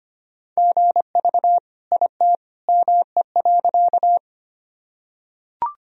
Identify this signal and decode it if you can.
Morse code, heard as a keyed tone, about 25 wpm, 700 Hz. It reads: GV IT ME.
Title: Sentences from Top 200 Words and Abbreviations 25wpm